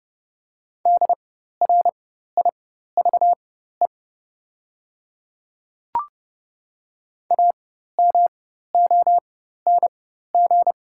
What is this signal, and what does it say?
Morse code, heard as a keyed tone, about 30 wpm, spaced as for 14 wpm, 700 Hz. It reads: DRIVE AMONG
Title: 4x Character Spacing - Top 500 Words - 30wpm